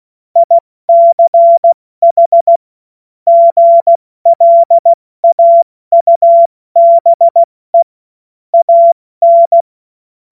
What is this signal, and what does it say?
Morse code, heard as a keyed tone, about 16 wpm, 680 Hz. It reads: ICH GLAUBE AN
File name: Morsekurs / Witze / Witz 09. Februar 2021, 70 F80